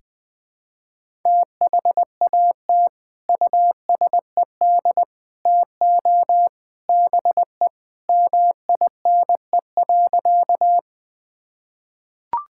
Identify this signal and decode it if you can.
Morse code, heard as a keyed tone, about 21 wpm, 700 Hz. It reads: THAT USED TO BE MINE.
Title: Sentences with 3-Word Phrases - 20wpm